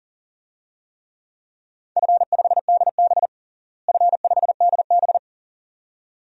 Morse code, heard as a keyed tone, about 40 wpm, 700 Hz. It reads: F5DB F5DB
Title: DXCC-100 Call Signs - 4 Characters - Repeated 2x - 40wpm